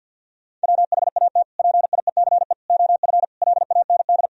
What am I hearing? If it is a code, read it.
Morse code, heard as a keyed tone, about 52 wpm, 700 Hz. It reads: WHAT PIECE OF LAND
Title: Sentences from Top 1000 Words - Review Part 2 - 50wpm